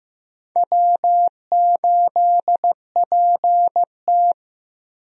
Morse code, heard as a keyed tone, about 15 wpm, 700 Hz. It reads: W8PT
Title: USA Call Signs - 4 Characters - 15wpm